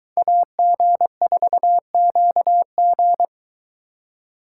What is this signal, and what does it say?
Morse code, heard as a keyed tone, about 22 wpm, 700 Hz. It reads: AG4QG